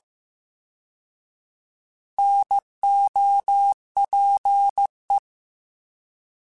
Morse code, heard as a keyed tone, about 15 wpm, 800 Hz. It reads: NOPE